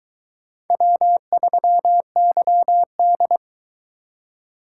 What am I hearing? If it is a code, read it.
Morse code, heard as a keyed tone, about 20 wpm, 700 Hz. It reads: W3YD